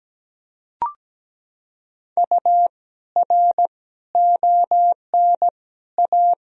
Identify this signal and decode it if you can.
Morse code, heard as a keyed tone, about 17 wpm, 700 Hz. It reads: U R ON A
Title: Sentences from Top 400 Words and Abbreviations 17wpm